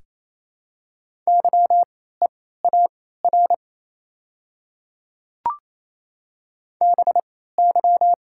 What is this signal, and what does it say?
Morse code, heard as a keyed tone, about 27 wpm, spaced as for 16 wpm, 690 Hz. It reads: YEAR BY
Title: 3x Character Spacing - Top 300 Words - 28wpm